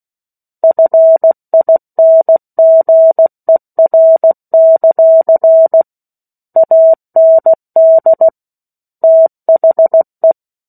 Morse code, heard as a keyed tone, about 16 wpm, 650 Hz. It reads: FINGER; AND THE
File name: 03-Chapter-03 16 WPM